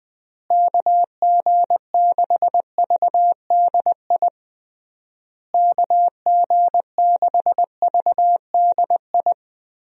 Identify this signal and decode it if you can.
Morse code, heard as a keyed tone, about 21 wpm, 700 Hz. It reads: KG6VDI KG6VDI
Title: USA Call Signs - 6 Characters - Repeated 3x - 20wpm